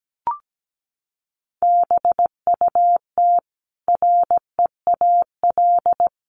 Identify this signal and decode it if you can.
Morse code, heard as a keyed tone, about 17 wpm, 700 Hz. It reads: BUT REAL
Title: Sentences from Top 1400 Words 17wpm